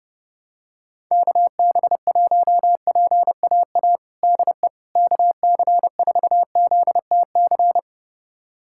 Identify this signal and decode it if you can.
Morse code, heard as a keyed tone, about 30 wpm, 700 Hz. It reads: KB1PAA DE KC4ZTC
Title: USA Callsign DE Callsign - 6 Characters - 30wpm